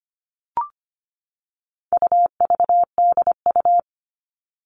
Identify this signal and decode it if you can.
Morse code, heard as a keyed tone, about 25 wpm, 700 Hz. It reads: UVDU
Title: All Cut Numbers - 4-Digit 25wpm